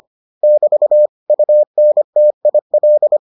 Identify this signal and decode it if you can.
Morse code, heard as a keyed tone, about 25 wpm, 600 Hz. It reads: <BT> UNTIL